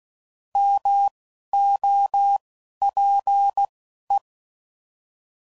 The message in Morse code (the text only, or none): MOPE